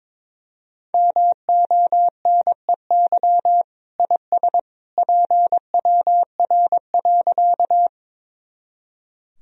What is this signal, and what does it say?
Morse code, heard as a keyed tone, about 21 wpm, 700 Hz. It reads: MONEY IS PWR.
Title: Sentences from Top 500 Words and Abbreviations 22wpm